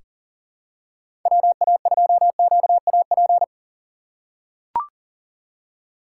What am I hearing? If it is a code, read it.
Morse code, heard as a keyed tone, about 37 wpm, 700 Hz. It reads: WA2QAP
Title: USA Call Signs - 6 Characters - 4x Character Spacing - 40wpm